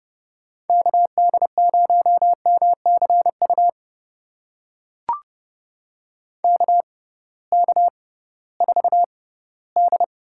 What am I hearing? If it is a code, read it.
Morse code, heard as a keyed tone, about 32 wpm, spaced as for 18 wpm, 700 Hz. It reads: KD0MCU KK4D
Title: USA Call Signs - 6 Characters - 6x Character Spacing - 30wpm